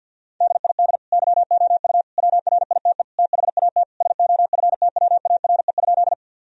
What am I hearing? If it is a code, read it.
Morse code, heard as a keyed tone, about 50 wpm, 700 Hz. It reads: DID YOU WRITE THAT SOFTWARE?
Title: Sentences from Top 1000 Words - Review Part 3 - 50wpm